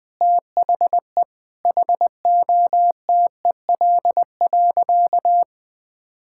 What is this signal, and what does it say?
Morse code, heard as a keyed tone, about 21 wpm, 700 Hz. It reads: THE HOTEL.
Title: Sentences from Top 1600 Words 20wpm